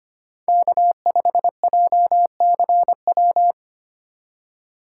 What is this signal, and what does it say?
Morse code, heard as a keyed tone, about 28 wpm, 700 Hz. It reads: K5JCW
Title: Speed Racing - USA 1x3 Call Signs 25wpm